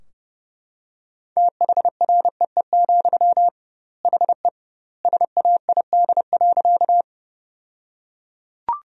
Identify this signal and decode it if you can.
Morse code, heard as a keyed tone, about 30 wpm, 700 Hz. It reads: THREE, HE SAID.
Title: 2x Word-Spacing - Sentences from Top 200 Words - 30wpm